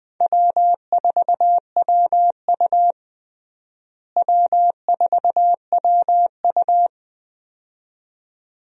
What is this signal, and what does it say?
Morse code, heard as a keyed tone, about 20 wpm, 700 Hz. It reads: W4WU W4WU
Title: USA Call Signs - 4 Characters - Repeated 2x - 20wpm